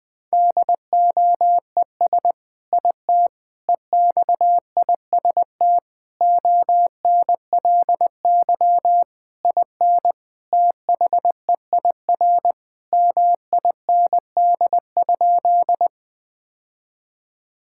Morse code, heard as a keyed tone, about 20 wpm, 700 Hz. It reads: DOES IT EXIST ONLY IN THEIR MIND?
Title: Sentences from Top 1000 Words - Review Part 5 - 20wpm